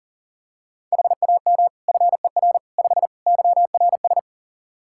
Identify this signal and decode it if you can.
Morse code, heard as a keyed tone, about 40 wpm, 700 Hz. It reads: HAM FER 5 YRS
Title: Practice QSO - Exchange 3 - More Info - 40wpm